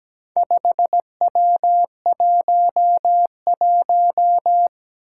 Morse code, heard as a keyed tone, about 17 wpm, 700 Hz. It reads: 5W11
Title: Four Challenging Letter-Number Combinations 17wpm